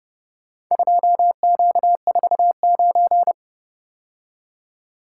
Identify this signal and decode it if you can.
Morse code, heard as a keyed tone, about 30 wpm, 700 Hz. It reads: 2Q49